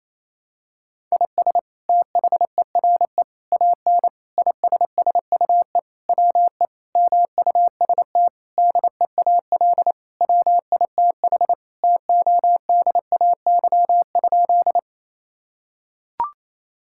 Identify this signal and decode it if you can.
Morse code, heard as a keyed tone, about 28 wpm, 700 Hz. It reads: IS THERE AN ISSUE WE MUST DEAL WITH TODAY?
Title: Sentences from Top 1000 Words - Review Part 6 - 28wpm